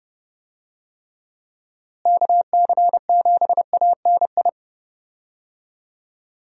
Morse code, heard as a keyed tone, about 31 wpm, 700 Hz. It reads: KC7ANI
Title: USA Call Signs - 6 Characters - Repeated 3x - 30wpm